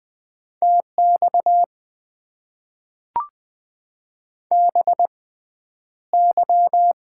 Morse code, heard as a keyed tone, about 20 wpm, spaced as for 10 wpm, 700 Hz. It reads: TX BY
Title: Top 90 QSO Elements - 6x Character Spacing - 20wpm